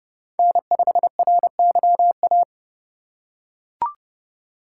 Morse code, heard as a keyed tone, about 30 wpm, 700 Hz. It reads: N5RYA
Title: USA Call Signs - 5 Characters - 3x Character Spacing - 30wpm